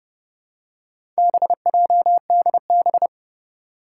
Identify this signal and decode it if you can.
Morse code, heard as a keyed tone, about 30 wpm, 700 Hz. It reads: BJDB
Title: Four Challenging Letter-Number Combinations 30wpm